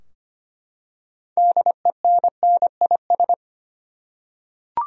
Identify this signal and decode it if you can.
Morse code, heard as a keyed tone, about 25 wpm, 700 Hz. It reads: DENNIS
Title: USA Common Names - 25wpm